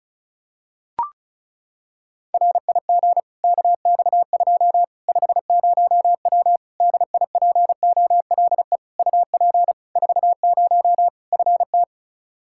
Morse code, heard as a keyed tone, about 36 wpm, 700 Hz. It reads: RIG KX2 50W DIPOLE UP 40 FT
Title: Practice QSO - Exchange 2 - Info - 35wpm